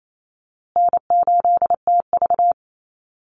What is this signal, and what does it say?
Morse code, heard as a keyed tone, about 28 wpm, 700 Hz. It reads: N8TV
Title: USA Call Signs Repeated 3x 28wpm